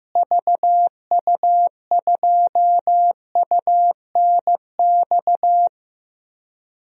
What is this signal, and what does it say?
Morse code, heard as a keyed tone, about 15 wpm, 700 Hz. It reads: VU2UNX
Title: DXCC-100 Call Signs - 6 Characters - Repeated 3x - 15wpm